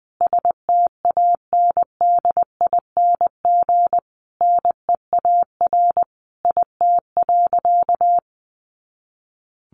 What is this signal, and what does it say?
Morse code, heard as a keyed tone, about 20 wpm, 700 Hz. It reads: STANDING NEAR IT.
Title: A Sound Of Thunder 20wpm